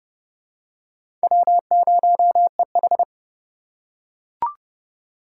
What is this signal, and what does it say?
Morse code, heard as a keyed tone, about 29 wpm, 700 Hz. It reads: W0EH